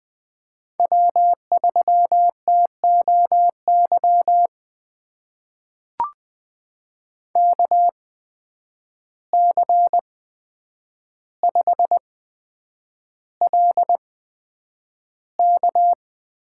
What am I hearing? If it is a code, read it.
Morse code, heard as a keyed tone, about 20 wpm, spaced as for 9 wpm, 700 Hz. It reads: W3TOY KC5LK
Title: USA Call Signs - 8x Character Spacing - 20wpm